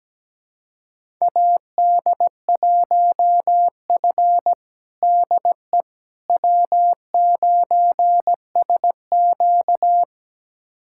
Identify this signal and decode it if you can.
Morse code, heard as a keyed tone, about 17 wpm, 700 Hz. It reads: AD1F DE W9SQ
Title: USA Callsign DE Callsign - 4 Characters - 17wpm